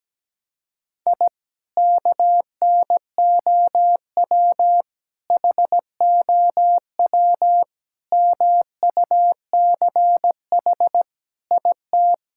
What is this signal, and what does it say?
Morse code, heard as a keyed tone, about 17 wpm, 700 Hz. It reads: I KNOW HOW MUCH IT
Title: Sentences from Top 1000 Words - Review Part 5 - 17wpm